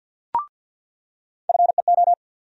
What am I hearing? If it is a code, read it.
Morse code, heard as a keyed tone, about 47 wpm, 700 Hz. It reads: FEO
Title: USA Call Sign Postfixes - 3 Letters 50wpm